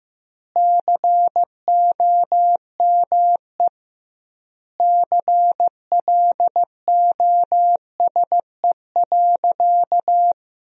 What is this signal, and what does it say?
Morse code, heard as a keyed tone, about 15 wpm, 700 Hz. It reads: COME CLOSE.
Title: 2x Word-Spacing - Sentences from Top 300 Words - 15wpm